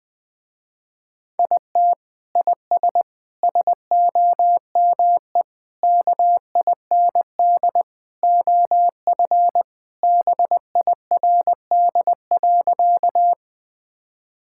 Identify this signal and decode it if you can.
Morse code, heard as a keyed tone, about 20 wpm, 700 Hz. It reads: IT IS SOME KIND OF BIRD.
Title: Sentences with 3-Word Phrases - Spoken Beginning - 20wpm